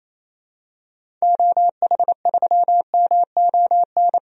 Morse code, heard as a keyed tone, about 28 wpm, 700 Hz. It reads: OH3MON
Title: DXCC-100 Call Signs - 6 Characters - 28wpm